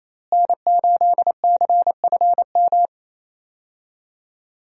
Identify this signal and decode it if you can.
Morse code, heard as a keyed tone, about 28 wpm, 700 Hz. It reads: N8CFM
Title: USA Call Signs - 5 Characters - Repeated 3x - 28wpm